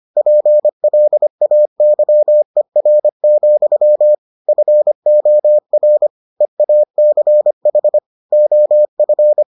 Morse code, heard as a keyed tone, about 25 wpm, 600 Hz. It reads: PLAYER, FOR EACH OF